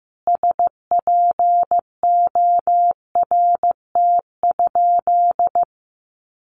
Morse code, heard as a keyed tone, about 15 wpm, 700 Hz. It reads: SPORT?
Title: Sentences from Top 1500 Words 15wpm